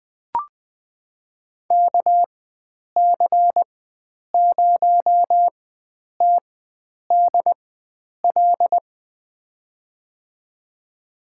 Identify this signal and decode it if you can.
Morse code, heard as a keyed tone, about 20 wpm, spaced as for 9 wpm, 700 Hz. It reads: KC0TDL
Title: USA Call Signs - 6 Characters - 4x Character Spacing - 20wpm